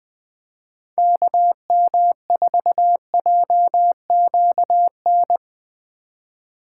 Morse code, heard as a keyed tone, about 20 wpm, 700 Hz. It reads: KM4JQN